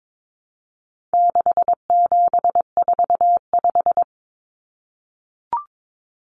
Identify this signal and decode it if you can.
Morse code, heard as a keyed tone, about 22 wpm, 700 Hz. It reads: 6745
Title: Common Cut Numbers - 4-Digit 22wpm